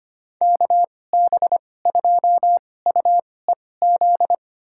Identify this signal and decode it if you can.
Morse code, heard as a keyed tone, about 25 wpm, spaced as for 18 wpm, 700 Hz. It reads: KB2UEZ
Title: USA Call Signs - 6 Characters - 2x Character Spacing - 25wpm